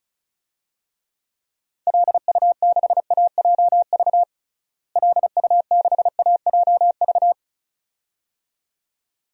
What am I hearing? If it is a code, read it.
Morse code, heard as a keyed tone, about 34 wpm, 700 Hz. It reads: LU6AJV LU6AJV
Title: DXCC-100 Call Signs - 6 Characters - Repeated 2x - 35wpm